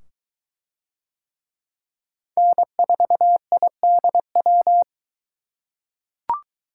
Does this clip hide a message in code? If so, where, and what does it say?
Yes: Morse code from 2.4 to 4.8 s, heard as a keyed tone, about 22 wpm, 700 Hz. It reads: N4IDW